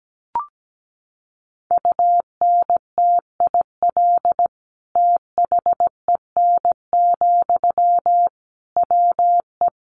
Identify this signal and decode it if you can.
Morse code, heard as a keyed tone, about 17 wpm, 700 Hz. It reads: UNTIL THEN, WE
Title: Sentences from Top 1500 Words 17wpm